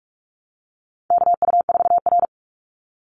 Morse code, heard as a keyed tone, about 44 wpm, 700 Hz. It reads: KU4R
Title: NA SOTA Chaser Callsigns 45wpm